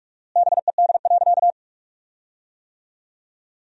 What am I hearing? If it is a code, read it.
Morse code, heard as a keyed tone, about 42 wpm, 700 Hz. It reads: BED.
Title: 4x Word-Spacing - Sentences from Top 500 Words - 45wpm